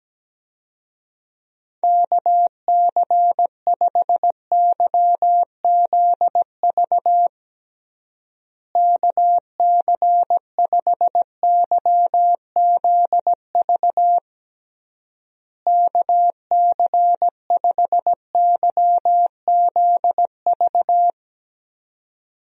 Morse code, heard as a keyed tone, about 17 wpm, 700 Hz. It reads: KC5YZV KC5YZV KC5YZV
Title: USA Call Signs - 6 Characters - Repeated 3x - 17wpm